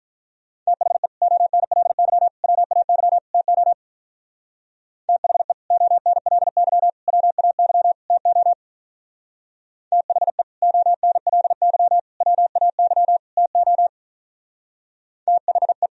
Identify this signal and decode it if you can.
Morse code, heard as a keyed tone, about 42 wpm, 700 Hz. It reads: THE ONLY WAY TO THE ONLY WAY TO THE ONLY WAY TO THE
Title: Speed-Racing - Sets of 4 Words 35wpm